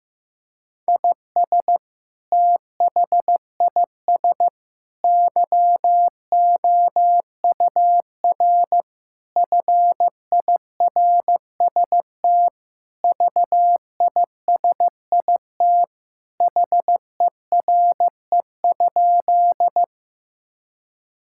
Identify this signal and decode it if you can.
Morse code, heard as a keyed tone, about 16 wpm, 700 Hz. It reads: IS THIS YOUR FIRST VISIT HERE?
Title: Sentences from Top 1000 Words - Review Part 6 - 15wpm